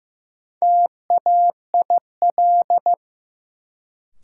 Morse code, heard as a keyed tone, about 15 wpm, 700 Hz. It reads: TAIL
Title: Sentences from Top 1000 Words - Review Part 3 - New Word Drill - 15wpm